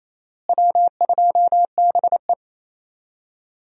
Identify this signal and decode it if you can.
Morse code, heard as a keyed tone, about 28 wpm, 700 Hz. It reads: W2BE
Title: USA Call Signs - 4 Characters - Repeated 2x - 28wpm